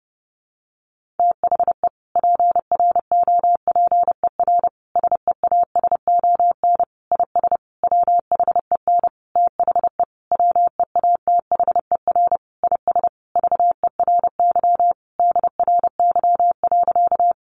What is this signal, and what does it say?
Morse code, heard as a keyed tone, about 30 wpm, 700 Hz. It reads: THE PROPER SEASON IS WHEN THE WEATHER IS VERY DRY.
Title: Sentences from Top 1400 Words 30wpm